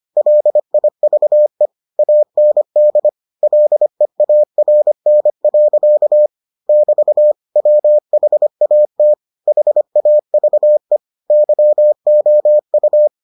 Morse code, heard as a keyed tone, about 25 wpm, 600 Hz. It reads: LIVE AND LEARN. <BT> WHAT HAVE YOU